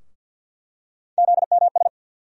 Morse code, heard as a keyed tone, about 47 wpm, 700 Hz. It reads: ZMS
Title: USA Call Sign Postfixes - 3 Letters 50wpm